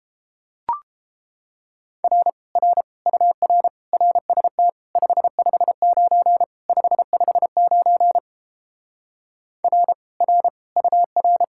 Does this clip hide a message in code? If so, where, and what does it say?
Yes: Morse code from 2.0 to 11.4 s, heard as a keyed tone, about 29 wpm, 700 Hz. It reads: R R UR RST 559 559 R R UR